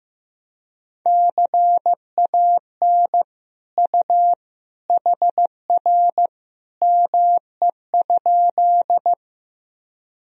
Morse code, heard as a keyed tone, about 15 wpm, 700 Hz. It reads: CAN U HR ME?
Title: Sentences from Top 300 Words and Abbreviations 15wpm